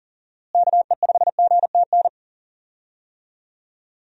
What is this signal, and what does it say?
Morse code, heard as a keyed tone, about 37 wpm, 700 Hz. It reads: KE5GTN
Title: USA Call Signs - 6 Characters - Repeated 2x - 40wpm